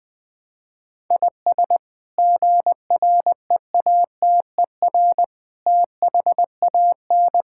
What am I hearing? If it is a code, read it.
Morse code, heard as a keyed tone, about 20 wpm, 700 Hz. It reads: IS GREATER THAN
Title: Sets of 8 Words - 20wpm